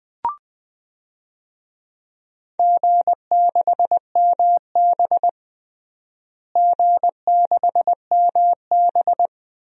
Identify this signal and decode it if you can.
Morse code, heard as a keyed tone, about 20 wpm, 700 Hz. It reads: G6MB G6MB
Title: DXCC-100 Call Signs - 4 Characters - Repeated 2x - 20wpm